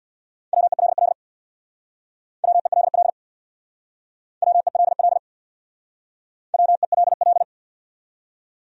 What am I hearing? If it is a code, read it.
Morse code, heard as a keyed tone, about 62 wpm, 700 Hz. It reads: WELL WELL WELL WELL
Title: Speed-Racing Top 200 Words 50wpm